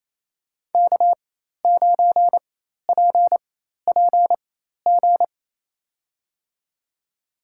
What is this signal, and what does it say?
Morse code, heard as a keyed tone, about 28 wpm, spaced as for 13 wpm, 700 Hz. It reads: K9PPG
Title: USA Call Signs - 5 Characters - 4x Character Spacing - 28wpm